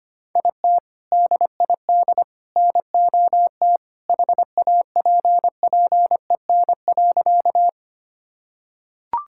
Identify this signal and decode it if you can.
Morse code, heard as a keyed tone, about 25 wpm, 700 Hz. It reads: IT DID NOT HAPPEN.
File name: practice / 25wpm / Sentences with 3-Word Phrases - 25wpm